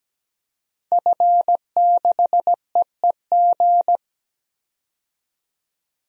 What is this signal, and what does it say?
Morse code, heard as a keyed tone, about 17 wpm, 700 Hz. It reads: F6EEG
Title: DXCC-100 Call Signs - 5 Characters - 17wpm